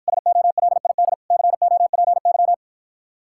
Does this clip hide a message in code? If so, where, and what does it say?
Yes: Morse code from 0.1 to 2.5 s, heard as a keyed tone, about 54 wpm, 700 Hz. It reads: SOLID COPY